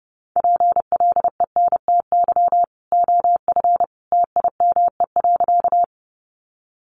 Morse code, heard as a keyed tone, about 30 wpm, 700 Hz. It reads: PLENTY OF TIME.
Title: Sentences from Top 1300 Words 30wpm